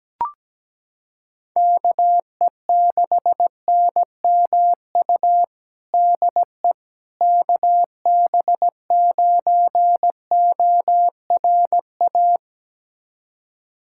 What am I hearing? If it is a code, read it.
Morse code, heard as a keyed tone, about 17 wpm, 700 Hz. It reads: KE6NMU DE KB9ORA
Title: USA Callsign DE Callsign 17wpm